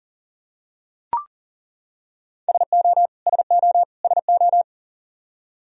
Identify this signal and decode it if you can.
Morse code, heard as a keyed tone, about 40 wpm, 700 Hz. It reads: SO SO SO
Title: Top 100 Words Repeated 40wpm